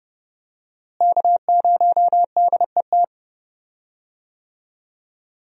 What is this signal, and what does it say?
Morse code, heard as a keyed tone, about 30 wpm, 700 Hz. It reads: K0DET
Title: USA Call Signs - 5 Characters - Repeated 3x - 30wpm